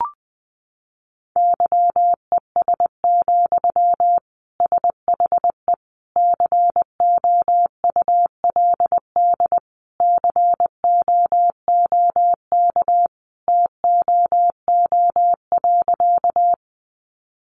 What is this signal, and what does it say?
Morse code, heard as a keyed tone, about 20 wpm, 700 Hz. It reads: YES, SHE COULD COOK TOO.
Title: Sentences from Top 700 Words 20wpm